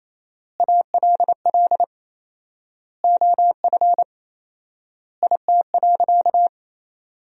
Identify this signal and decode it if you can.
Morse code, heard as a keyed tone, about 28 wpm, 700 Hz. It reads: ALL OF IT.
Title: 4x Word-Spacing - Sentences from Top 500 Words - 28wpm